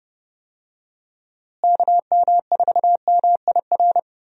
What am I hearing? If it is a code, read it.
Morse code, heard as a keyed tone, about 30 wpm, 700 Hz. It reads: KM4MIR
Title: USA Call Signs - 6 Characters - Repeated 2x - 30wpm